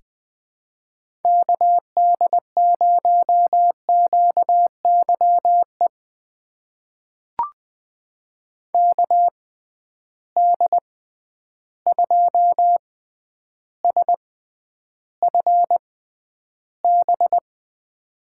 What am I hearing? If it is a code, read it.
Morse code, heard as a keyed tone, about 20 wpm, spaced as for 10 wpm, 700 Hz. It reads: KD0QYE KD2SFB
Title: USA Call Signs - 6 Characters - 6x Character Spacing - 20wpm